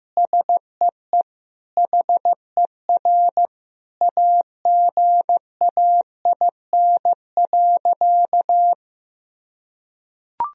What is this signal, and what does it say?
Morse code, heard as a keyed tone, about 15 wpm, 700 Hz. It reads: SEE HER AGAIN.
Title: Sentences from Top 1000 Words - Review Part 3 - 15wpm